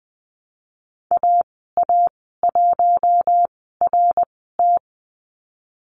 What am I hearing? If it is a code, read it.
Morse code, heard as a keyed tone, about 20 wpm, spaced as for 15 wpm, 700 Hz. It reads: AA1RT
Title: 088 Lesson 15g - Number 5 - Call Signs 20wpm